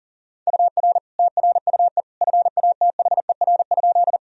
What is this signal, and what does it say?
Morse code, heard as a keyed tone, about 37 wpm, 700 Hz. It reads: UR TRUE FATHER?
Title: Sentences from Top 400 Words and Abbreviations 40wpm